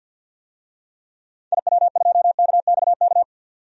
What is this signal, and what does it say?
Morse code, heard as a keyed tone, about 45 wpm, 700 Hz. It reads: IW2KXK
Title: DXCC-100 Call Signs - 6 Characters - 50wpm